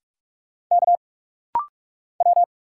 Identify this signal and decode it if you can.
Morse code, heard as a keyed tone, about 46 wpm, 700 Hz. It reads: K W
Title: Single Letter-Number Alt - Rapid-Fire 45wpm